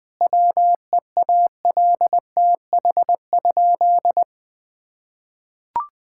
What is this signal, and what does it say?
Morse code, heard as a keyed tone, about 20 wpm, 700 Hz. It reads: WEALTH?
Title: Sentences from Top 1600 Words 20wpm